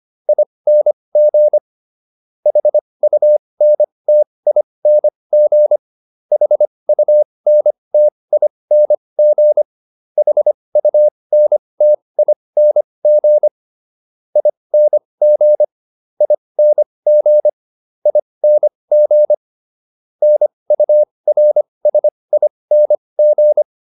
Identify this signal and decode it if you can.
Morse code, heard as a keyed tone, about 25 wpm, spaced as for 20 wpm, 600 Hz. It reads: ING HUNTING HUNTING HUNTING ING ING ING NURSING